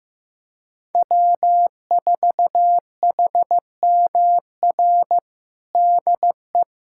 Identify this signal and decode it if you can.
Morse code, heard as a keyed tone, about 15 wpm, 700 Hz. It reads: W4HMR DE
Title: USA Callsign DE Callsign - 5 Characters - 15wpm